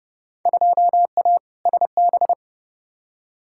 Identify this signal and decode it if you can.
Morse code, heard as a keyed tone, about 30 wpm, 700 Hz. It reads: 2A SB
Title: Field Day Exchanges - 30wpm